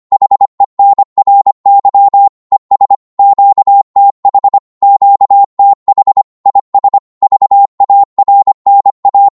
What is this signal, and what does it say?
Morse code, heard as a keyed tone, about 25 wpm, 800 Hz. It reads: HENRY ES QTH QTH IS VARNA
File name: Morse / qso-25wpm